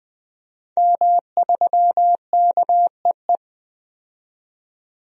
Morse code, heard as a keyed tone, about 21 wpm, 700 Hz. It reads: M3KEE
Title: DXCC-100 Call Signs - 5 Characters - 20wpm